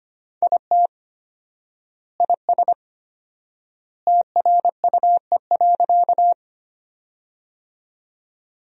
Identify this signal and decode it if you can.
Morse code, heard as a keyed tone, about 25 wpm, 700 Hz. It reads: IT IS TRUE.
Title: 4x Word-Spacing - Sentences from Top 400 Words - 25wpm